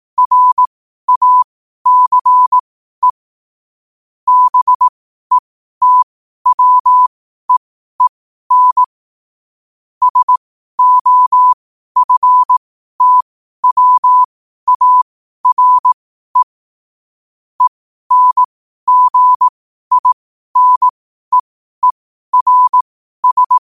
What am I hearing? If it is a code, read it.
Morse code, heard as a keyed tone, about 18 wpm, spaced as for 12 wpm, 1000 Hz. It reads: RACE BETWEEN SOFTWARE ENGINEERS
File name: qotdfiles / 12wpm